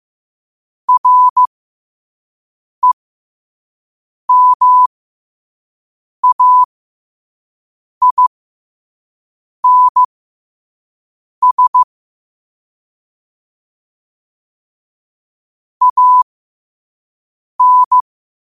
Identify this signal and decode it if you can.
Morse code, heard as a keyed tone, about 15 wpm, spaced as for 5 wpm, 1000 Hz. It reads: REMAINS AN
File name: qotdfiles / 5wpm